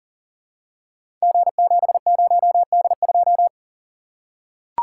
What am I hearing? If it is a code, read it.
Morse code, heard as a keyed tone, about 42 wpm, 700 Hz. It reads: G70D2
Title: Code Groups - 5 Letter:Numbers - 40wpm